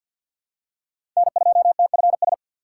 Morse code, heard as a keyed tone, about 53 wpm, 700 Hz. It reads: N2TFS
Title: USA Call Signs - 5 Characters - 8x Character Spacing - 50wpm